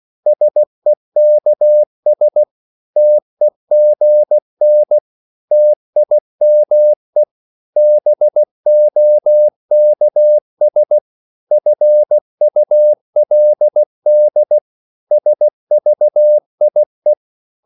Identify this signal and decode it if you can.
Morse code, heard as a keyed tone, about 16 wpm, 600 Hz. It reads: SEKS TEGN TIME BOKS FULD SVIE
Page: Korte danske ord 16wpm